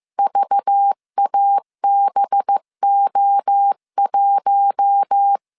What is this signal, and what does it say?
Morse code, heard as a keyed tone, about 15 wpm, 800 Hz. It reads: VABO1
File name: CW / DET / 1 6 1500C S B 14wpm 2958